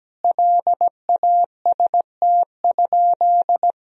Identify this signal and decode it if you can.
Morse code, heard as a keyed tone, about 17 wpm, 700 Hz. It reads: LAST?
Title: Sentences from Top 1000 Words - Review Part 4 - 17wpm